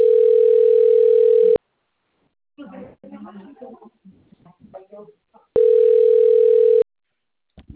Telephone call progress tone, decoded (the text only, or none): ringback tone